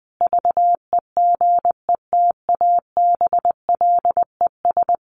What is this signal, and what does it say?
Morse code, heard as a keyed tone, about 19 wpm, 700 Hz. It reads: VEGETABLES
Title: Easy Phrases 20wpm